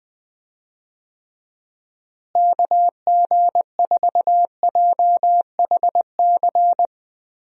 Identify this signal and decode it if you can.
Morse code, heard as a keyed tone, about 20 wpm, 700 Hz. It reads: KG4JHC
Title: USA Call Signs - 6 Characters - Repeated 2x - 20wpm